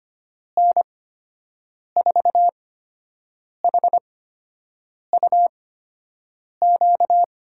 Morse code, heard as a keyed tone, about 25 wpm, spaced as for 7 wpm, 700 Hz. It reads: N4HUQ
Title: USA Call Signs - 6 Characters - 8x Character Spacing - 25wpm